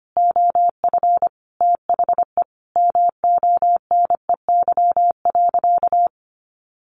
Morse code, heard as a keyed tone, about 25 wpm, 700 Hz. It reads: OF THE MONEY.
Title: Sentences from Top 1100 Words 25wpm